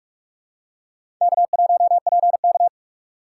Morse code, heard as a keyed tone, about 46 wpm, 700 Hz. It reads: K1PK
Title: USA Call Signs - 4 Characters - 6x Character Spacing - 45wpm